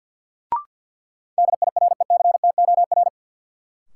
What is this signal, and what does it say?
Morse code, heard as a keyed tone, about 53 wpm, 700 Hz. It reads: DIRECTOR
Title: Sentences from Top 1300 Words - New Word Drill Alt - 50wpm